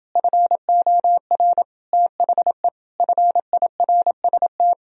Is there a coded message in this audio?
Yes: Morse code from 0.2 to 4.7 s, heard as a keyed tone, about 31 wpm, 700 Hz. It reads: FOR THE FIRST